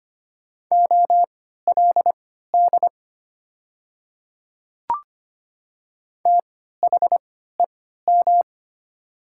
Morse code, heard as a keyed tone, about 25 wpm, spaced as for 14 wpm, 700 Hz. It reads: OLD THEM
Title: 3x Character Spacing - Top 200 Words - 25wpm